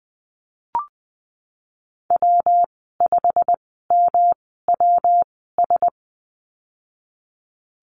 Morse code, heard as a keyed tone, about 19 wpm, spaced as for 14 wpm, 700 Hz. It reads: W5MWS
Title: 128 Lesson 21g - Number 6 - Call Signs 20wpm